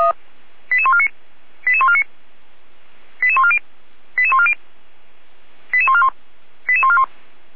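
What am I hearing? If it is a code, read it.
Five-tone selective call (ZVEI), heard as quick runs of five tones, tones of about 70 ms. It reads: 88149 88149 88140 88140 88141 88141
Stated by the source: ZVEI five-tone selective call received on SDR.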